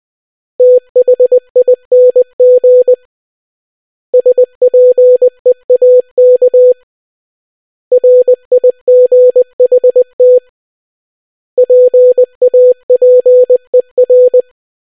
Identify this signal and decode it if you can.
Morse code, heard as a keyed tone, about 20 wpm, 500 Hz. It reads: THING SPEAK RIGHT PAPER